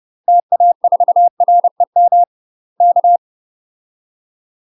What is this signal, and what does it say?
Morse code, heard as a keyed tone, about 30 wpm, 700 Hz. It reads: TA4REM K
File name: UK MASTER QSOs TRAINING / 30WPM / UK MASTER QSOs 0034